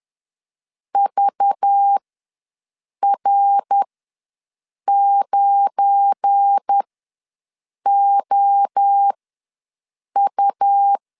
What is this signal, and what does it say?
Morse code, heard as a keyed tone, about 10 wpm, spaced as for 6 wpm, 800 Hz. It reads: VR9OU
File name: LEZIONI / 7 GF9Z / 7 GF9 ws 10wpm 4034